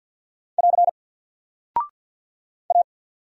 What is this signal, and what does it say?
Morse code, heard as a keyed tone, about 49 wpm, 700 Hz. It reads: <AR> A
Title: Single Character (Letter:Number:Punctuation:Prosign) 50wpm